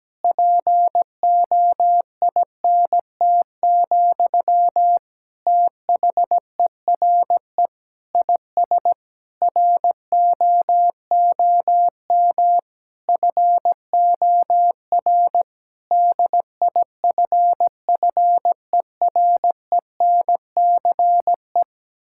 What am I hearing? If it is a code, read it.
Morse code, heard as a keyed tone, about 17 wpm, 700 Hz. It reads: POINT, THERE IS ROOM FOR DIFFERENCE
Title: Sentences from Top 1600 Words 17wpm